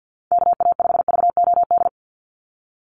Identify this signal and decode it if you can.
Morse code, heard as a keyed tone, about 49 wpm, 700 Hz. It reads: KA5VOD
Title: USA Call Signs 50wpm